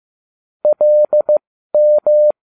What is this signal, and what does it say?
Morse code, heard as a keyed tone, about 15 wpm, spaced as for 12 wpm, 600 Hz. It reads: LM